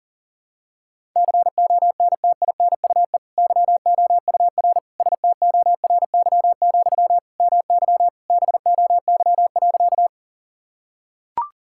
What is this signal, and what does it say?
Morse code, heard as a keyed tone, about 40 wpm, 700 Hz. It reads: CONTINUE YOUR STORY, MY BOY.
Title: Sentences from Top 1000 Words - Review Part 6 - 40wpm